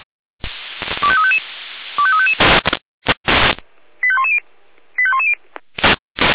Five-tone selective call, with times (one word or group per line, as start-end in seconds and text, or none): 1.0-1.4 25355
2.0-2.3 25355
4.0-4.4 85110
5.0-5.3 85110